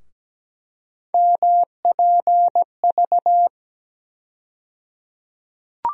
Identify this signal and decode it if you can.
Morse code, heard as a keyed tone, about 17 wpm, 700 Hz. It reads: MPV
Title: USA Call Sign Postfixes - 3 Letters 17wpm